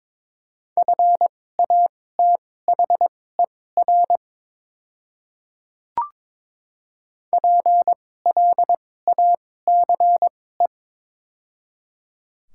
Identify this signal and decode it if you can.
Morse code, heard as a keyed tone, about 22 wpm, spaced as for 16 wpm, 700 Hz. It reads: FATHER PLACE
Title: 2x Character Spacing - Top 300 Words - 22wpm